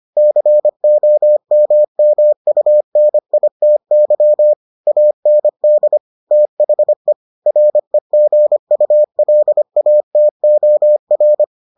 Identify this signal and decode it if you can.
Morse code, heard as a keyed tone, about 25 wpm, 600 Hz. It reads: COMMUNITY AND THE REGULATOR